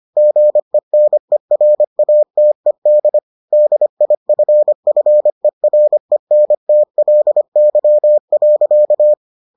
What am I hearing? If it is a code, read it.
Morse code, heard as a keyed tone, about 25 wpm, 600 Hz. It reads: GENERATED DIFFERENTLY.